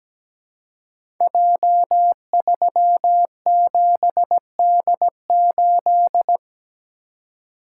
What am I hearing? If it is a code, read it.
Morse code, heard as a keyed tone, about 17 wpm, 700 Hz. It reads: J37D8
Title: Code Groups - 5 Letter:Numbers - 17wpm